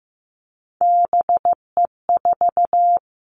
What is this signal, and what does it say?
Morse code, heard as a keyed tone, about 15 wpm, 700 Hz. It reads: BE4
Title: Three Challenging Letter-Number Combinations 15wpm